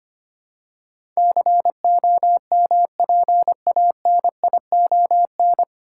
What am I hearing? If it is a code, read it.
Morse code, heard as a keyed tone, about 25 wpm, 700 Hz. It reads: COMPANION
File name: practice / 25wpm / 9 Letter Words - First 3 Letters Given - 25wpm